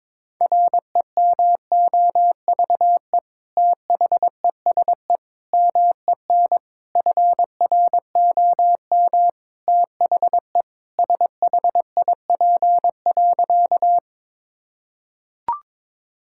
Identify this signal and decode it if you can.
Morse code, heard as a keyed tone, about 22 wpm, 700 Hz. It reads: REMOVE THESE MEN FROM THE SHIP.
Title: Sentences from Top 1000 Words - Review Part 5 - 22wpm